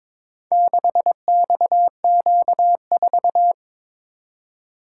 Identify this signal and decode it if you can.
Morse code, heard as a keyed tone, about 22 wpm, 700 Hz. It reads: 6XQ4
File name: practice / 22wpm / Four Letter-Number Combinations - 22wpm